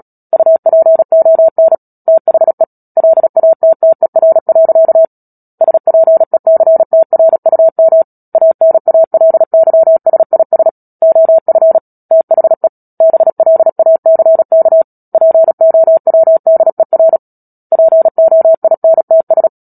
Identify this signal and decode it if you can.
Morse code, heard as a keyed tone, about 37 wpm, 650 Hz. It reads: UPON THE LATTER. SPECTRUM ANALYSIS OF THE BLACK POWDER POINTS
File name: Book Two / 10-Chapter 10 - Epilogue - 36 WPM